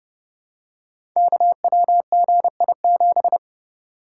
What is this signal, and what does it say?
Morse code, heard as a keyed tone, about 30 wpm, 700 Hz. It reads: KWGI7
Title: Code Groups - 5 Letter:Numbers - 30wpm